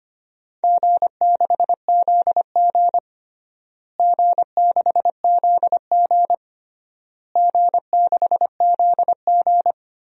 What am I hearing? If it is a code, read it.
Morse code, heard as a keyed tone, about 25 wpm, 700 Hz. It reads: G6ZG G6ZG G6ZG
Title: DXCC-100 Call Signs - 4 Characters - Repeated 3x - 25wpm